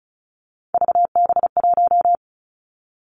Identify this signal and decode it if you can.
Morse code, heard as a keyed tone, about 36 wpm, 700 Hz. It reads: VB1
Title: Three Challenging Letter-Number Combinations 35wpm